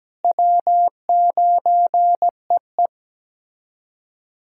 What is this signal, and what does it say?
Morse code, heard as a keyed tone, about 17 wpm, 700 Hz. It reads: W9EE
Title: USA Callsign DE Callsign - 4 Characters - 17wpm